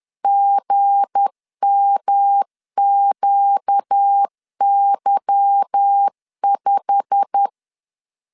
Morse code, heard as a keyed tone, about 10 wpm, 800 Hz. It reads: GMQY5
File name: LEZIONI / 8 YJ45 / 8 YJ45 10wpm 2549